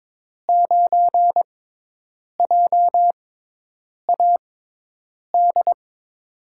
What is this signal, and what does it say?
Morse code, heard as a keyed tone, about 21 wpm, spaced as for 8 wpm, 700 Hz. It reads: 9JAD
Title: USA Call Signs - 6 Characters - 6x Character Spacing - 22wpm